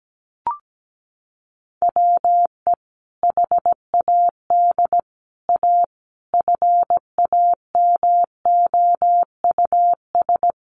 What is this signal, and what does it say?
Morse code, heard as a keyed tone, about 17 wpm, 700 Hz. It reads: WE HAD A FAMOUS
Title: Sentences from Top 1500 Words 17wpm